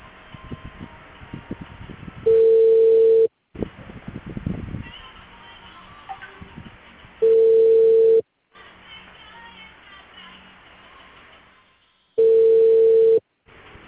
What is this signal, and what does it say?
Ringback tone (China, 450 Hz).